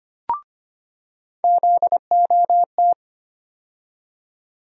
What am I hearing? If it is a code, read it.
Morse code, heard as a keyed tone, about 25 wpm, 700 Hz. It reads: ZOT
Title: USA Call Sign Postfixes - 3 Letters 25wpm